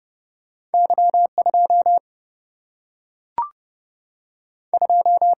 Morse code, heard as a keyed tone, about 31 wpm, 700 Hz. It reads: Y 2 2
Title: Single-Letter Single-Number 30wpm